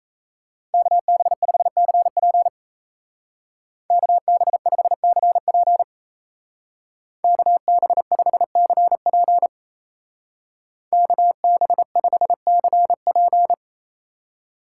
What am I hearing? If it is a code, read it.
Morse code, heard as a keyed tone, about 33 wpm, 700 Hz. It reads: KB5CP KB5CP KB5CP KB5CP
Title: Speed Racing - USA 2x2 Call Signs 28wpm